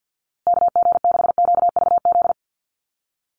Morse code, heard as a keyed tone, about 50 wpm, 700 Hz. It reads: KG6QVZ
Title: USA Call Signs Repeated 3x 50wpm